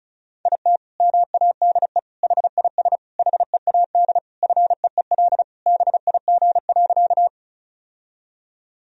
Morse code, heard as a keyed tone, about 34 wpm, 700 Hz. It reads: IT MADE HIS HEAD FEEL BIG.
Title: Sentences from Top 1000 Words - Review Part 1 - 35wpm